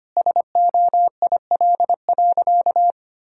Morse code, heard as a keyed tone, about 25 wpm, 700 Hz. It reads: SOIL.
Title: Sentences from Top 1000 Words - Review Part 4 - 25wpm